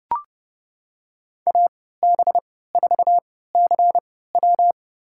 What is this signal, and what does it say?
Morse code, heard as a keyed tone, about 30 wpm, spaced as for 17 wpm, 700 Hz. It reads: AB4CW
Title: USA Call Signs - 5 Characters - 3x Character Spacing - 30wpm